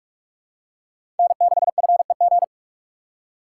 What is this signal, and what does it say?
Morse code, heard as a keyed tone, about 42 wpm, 700 Hz. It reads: N6FEG
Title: USA Call Signs - 5 Characters - 4x Character Spacing - 45wpm